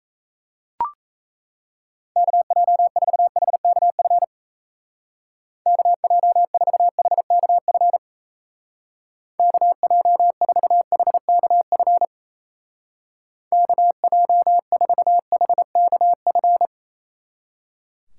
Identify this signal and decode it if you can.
Morse code, heard as a keyed tone, about 35 wpm, 700 Hz. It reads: KJ4HKF KJ4HKF KJ4HKF KJ4HKF